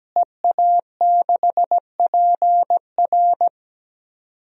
Morse code, heard as a keyed tone, about 17 wpm, 700 Hz. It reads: EA6PR
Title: DXCC-100 Call Signs - 5 Characters - Repeated 3x - 17wpm